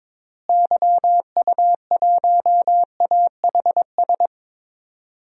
Morse code, heard as a keyed tone, about 22 wpm, 700 Hz. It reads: YU1AHS
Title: DXCC-100 Call Signs - 6 Characters - Repeated 3x - 22wpm